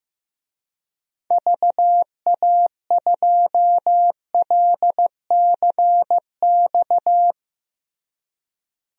Morse code, heard as a keyed tone, about 16 wpm, 700 Hz. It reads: VA2LCX
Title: DXCC-100 Call Signs - 6 Characters - Repeated 2x - 15wpm